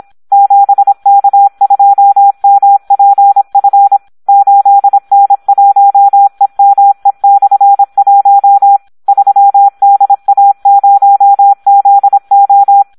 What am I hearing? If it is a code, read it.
Morse code, heard as a keyed tone, about 26 wpm, 800 Hz, with radio noise. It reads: 7K2MPF 8N1EME/1 3DA0ZO
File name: Audio practica CW / SD 1 26